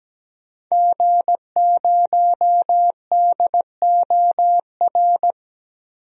Morse code, heard as a keyed tone, about 17 wpm, 700 Hz. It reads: G0DOR
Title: DXCC-100 Call Signs - 5 Characters - Repeated 2x - 17wpm